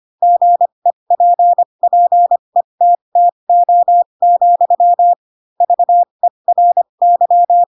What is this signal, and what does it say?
Morse code, heard as a keyed tone, about 25 wpm, spaced as for 22 wpm, 700 Hz. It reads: GEPPETTO, VERY
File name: Pinocchio / Pinocchio ch7 18 25 ik1hge